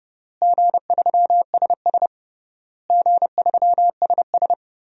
Morse code, heard as a keyed tone, about 30 wpm, 700 Hz. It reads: G3SS G3SS
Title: DXCC-100 Call Signs - 4 Characters - Repeated 3x - 30wpm